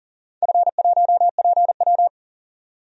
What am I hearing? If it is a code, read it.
Morse code, heard as a keyed tone, about 42 wpm, 700 Hz. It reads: F1PW
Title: DXCC-100 Call Signs - 4 Characters - 40wpm